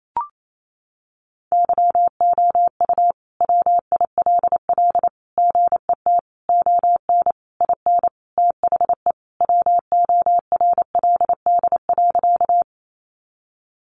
Morse code, heard as a keyed tone, about 28 wpm, 700 Hz. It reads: YOU WILL GET ON IN THE WORLD.
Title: Sentences from Top 200 Words 28wpm